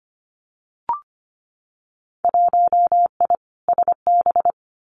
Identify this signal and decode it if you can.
Morse code, heard as a keyed tone, about 25 wpm, 700 Hz. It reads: 1I SB
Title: Winter Field Day Exchanges Encore 25wpm